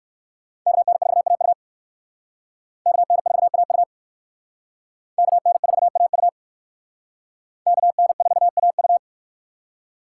Morse code, heard as a keyed tone, about 57 wpm, 700 Hz. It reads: KN4AU KN4AU KN4AU KN4AU